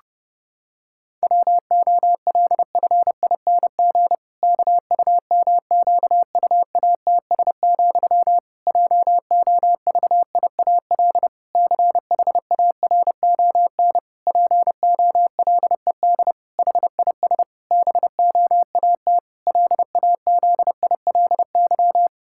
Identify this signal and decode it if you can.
Morse code, heard as a keyed tone, about 30 wpm, 700 Hz. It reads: WOLFING KUMQUATS, JOVIAL CHARON POLED HIS BOAT LAZILY
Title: Pangram Challenge v2 - 30wpm